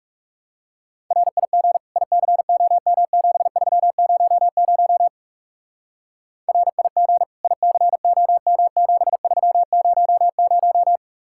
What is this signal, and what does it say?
Morse code, heard as a keyed tone, about 36 wpm, 700 Hz. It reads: RIG ICOM7300 RIG ICOM7300
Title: Speed-Racing Common QSO Phrases - 30wpm